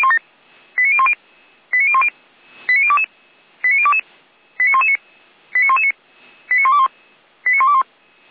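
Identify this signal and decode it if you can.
Five-tone selective call (ZVEI), heard as quick runs of five tones, tones of about 70 ms. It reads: 89010 89010 89022 89022 89119 89119 89121 89121
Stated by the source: ZVEI five-tone selective call received by SDR.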